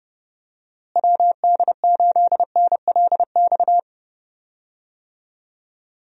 Morse code, heard as a keyed tone, about 30 wpm, 700 Hz. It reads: WD8NLX
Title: USA Call Signs - 6 Characters - Repeated 2x - 30wpm